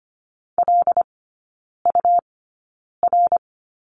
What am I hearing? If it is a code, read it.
Morse code, heard as a keyed tone, about 25 wpm, spaced as for 9 wpm, 700 Hz. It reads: LUR
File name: Lesson 17e - Letter u - Three Characters 25wpm